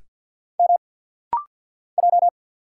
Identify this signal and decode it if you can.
Morse code, heard as a keyed tone, about 50 wpm, 700 Hz. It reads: M J
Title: Single Letter-Number Alt - Rapid-Fire 50wpm